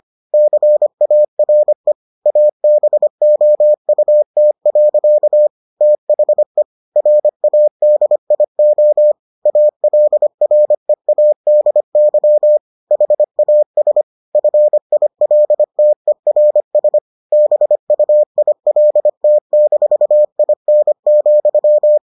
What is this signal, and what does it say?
Morse code, heard as a keyed tone, about 25 wpm, 600 Hz. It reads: CARE ABOUT. THE RADIO ALREADY HAS FILTERS BUILT-IN,